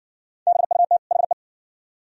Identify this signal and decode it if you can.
Morse code, heard as a keyed tone, about 60 wpm, 700 Hz. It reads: BUT HE